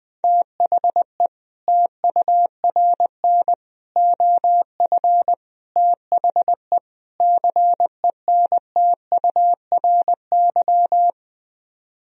Morse code, heard as a keyed tone, about 20 wpm, 700 Hz. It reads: THE TURN OF THE CENTURY
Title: Speed-Racing - Sets of 5 Words 20wpm